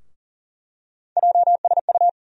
Morse code, heard as a keyed tone, about 40 wpm, 700 Hz. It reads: JSU
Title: USA Call Sign Postfixes - 3 Letters 40wpm